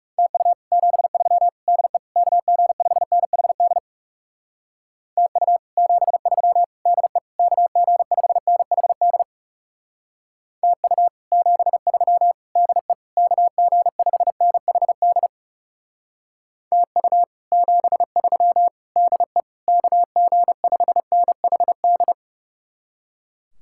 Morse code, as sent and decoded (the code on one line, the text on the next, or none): - ..- / --... ...-- / -.. / -.- --. ..... -. .... -.. / - ..- / --... ...-- / -.. / -.- --. ..... -. .... -.. / - ..- / --... ...-- / -.. / -.- --. ..... -. .... -.. / - ..- / --... ...-- / -.. / -.- --. ..... -. .... -..
TU 73 DE KG5NHD TU 73 DE KG5NHD TU 73 DE KG5NHD TU 73 DE KG5NHD